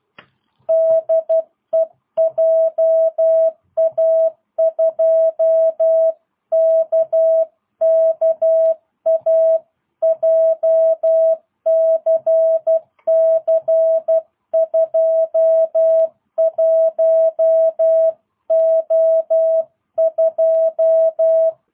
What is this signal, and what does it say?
Morse code, heard as a keyed tone, about 12 wpm, 660 Hz. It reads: DE JA2KKA JCC21O2
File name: jh8cbh / 2022cw